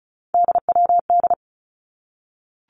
Morse code, heard as a keyed tone, about 34 wpm, 700 Hz. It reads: DWD
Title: Three Challenging Letter-Number Combinations 35wpm